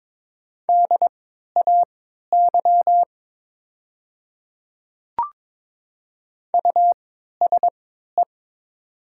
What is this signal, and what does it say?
Morse code, heard as a keyed tone, about 22 wpm, spaced as for 12 wpm, 700 Hz. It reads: DAY USE
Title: 3x Character Spacing - Top 200 Words - 22wpm